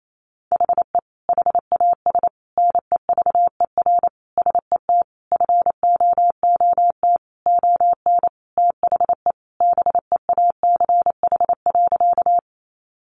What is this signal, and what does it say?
Morse code, heard as a keyed tone, about 28 wpm, 700 Hz. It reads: HE HAS NEVER SET FOOT ON THE BEACH.
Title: Sentences from Top 1500 Words 28wpm